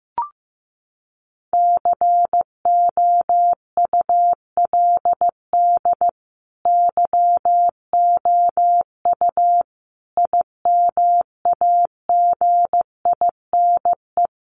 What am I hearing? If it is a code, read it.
Morse code, heard as a keyed tone, about 15 wpm, 700 Hz. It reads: COULD YOU IMAGINE
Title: Sentences from Top 1400 Words 15wpm